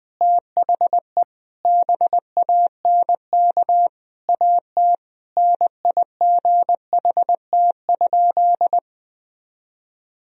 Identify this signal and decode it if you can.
Morse code, heard as a keyed tone, about 20 wpm, 700 Hz. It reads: THE BANK AT NIGHT?
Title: Sentences from Top 1000 Words - Review Part 4 - 20wpm